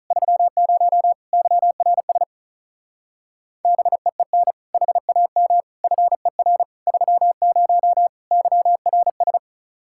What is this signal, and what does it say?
Morse code, heard as a keyed tone, about 44 wpm, 700 Hz. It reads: 30 YRS BEEN HAM FER 30 YRS